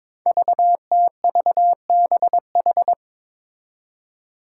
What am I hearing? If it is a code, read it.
Morse code, heard as a keyed tone, about 23 wpm, 700 Hz. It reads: VTVBH
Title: Five Challenging Letter-Number Combinations 22wpm